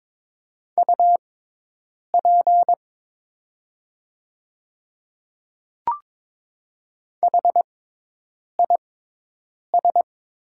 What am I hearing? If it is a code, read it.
Morse code, heard as a keyed tone, about 22 wpm, spaced as for 8 wpm, 700 Hz. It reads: UP HIS
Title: 6x Character Spacing - Top 100 Words - 22wpm